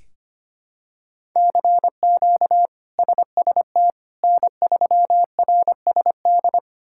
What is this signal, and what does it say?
Morse code, heard as a keyed tone, about 25 wpm, 700 Hz. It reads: CQ SST N3RSD
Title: K1USN SST QSO Practice - 25wpm